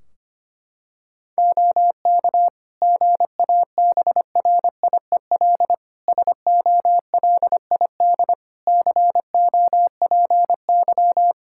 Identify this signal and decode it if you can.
Morse code, heard as a keyed tone, about 25 wpm, 700 Hz. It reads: OK GABRIEL SOLID COPY
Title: Practice QSO - Exchange 3 - More Info - 25wpm